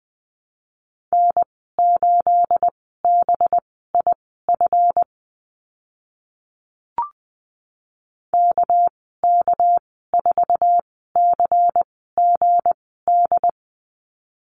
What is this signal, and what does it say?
Morse code, heard as a keyed tone, about 20 wpm, spaced as for 14 wpm, 700 Hz. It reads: N8BIF KK4CGD
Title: 213 Lesson 34g - Number 8 - Call Signs 20wpm